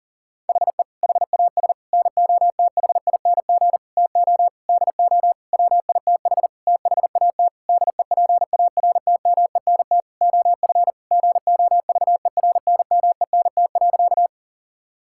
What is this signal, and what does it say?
Morse code, heard as a keyed tone, about 40 wpm, 700 Hz. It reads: HE HAS NOTHING TO DO WITH THAT DEPARTMENT OF GOVERNMENT.
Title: Sentences from Top 1000 Words - Review Part 5 - 40wpm